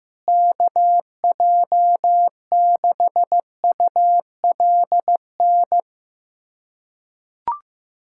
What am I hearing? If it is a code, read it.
Morse code, heard as a keyed tone, about 16 wpm, 700 Hz. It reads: KJ6ULN
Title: USA Callsign DE Callsign - 6 Characters - 15wpm